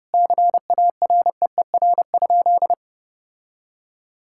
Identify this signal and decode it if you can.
Morse code, heard as a keyed tone, about 30 wpm, 700 Hz. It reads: CAREER?
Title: Sentences from Top 1000 Words - Review Part 6 - 30wpm